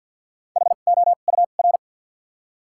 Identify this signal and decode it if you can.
Morse code, heard as a keyed tone, about 53 wpm, 700 Hz. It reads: H O U R
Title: 2x Character Spacing - Top 500 Words - 50wpm